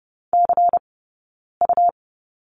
Morse code, heard as a keyed tone, about 30 wpm, 700 Hz. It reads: C U
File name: Lesson 19d - Letter w - Two Characters 30wpm